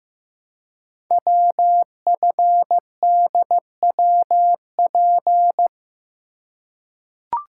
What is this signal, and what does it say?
Morse code, heard as a keyed tone, about 15 wpm, 700 Hz. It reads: WFDWP
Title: Code Groups - 5 Letters - 15wpm